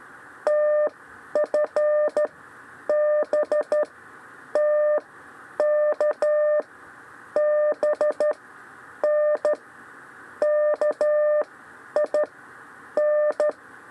Morse code, heard as a keyed tone, about 13 wpm, spaced as for 9 wpm, 600 Hz. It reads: TFBTKBN KIN
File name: 6 lezione cw 2